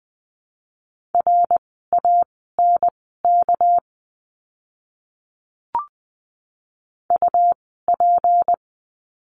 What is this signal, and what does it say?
Morse code, heard as a keyed tone, about 20 wpm, spaced as for 15 wpm, 700 Hz. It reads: RANK UP
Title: 199 Lesson 32f - Letter k - Words 20wpm